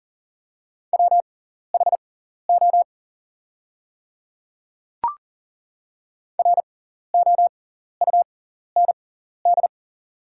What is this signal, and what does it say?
Morse code, heard as a keyed tone, about 40 wpm, spaced as for 14 wpm, 700 Hz. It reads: WHO ROUND
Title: 6x Character Spacing - Top 200 Words - 40wpm